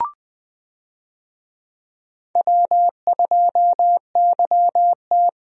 Morse code, heard as a keyed tone, about 20 wpm, 700 Hz. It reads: W2YT
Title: USA Call Signs - 4 Characters - Repeated 3x - 20wpm